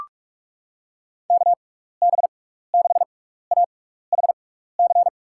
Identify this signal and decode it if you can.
Morse code, heard as a keyed tone, about 46 wpm, spaced as for 15 wpm, 700 Hz. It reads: KB6AHC
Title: USA Call Signs - 6 Characters - 6x Character Spacing - 45wpm